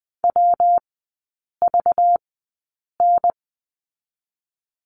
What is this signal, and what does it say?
Morse code, heard as a keyed tone, about 20 wpm, spaced as for 8 wpm, 690 Hz. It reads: WVN
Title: 191 Lesson 31e - Letter v - Three Characters 20wpm